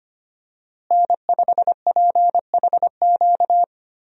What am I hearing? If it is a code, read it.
Morse code, heard as a keyed tone, about 25 wpm, 700 Hz. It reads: N5PHQ
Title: USA Call Signs - 5 Characters - Repeated 2x - 25wpm